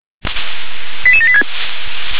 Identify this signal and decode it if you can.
Five-tone selective call (ZVEI), heard as quick runs of five tones, tones of about 70 ms. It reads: 88786